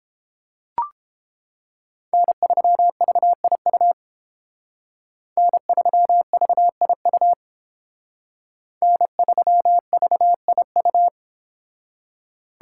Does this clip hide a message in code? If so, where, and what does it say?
Yes: Morse code from 2.1 to 11.1 s, heard as a keyed tone, about 31 wpm, 700 Hz. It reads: N3VIU N3VIU N3VIU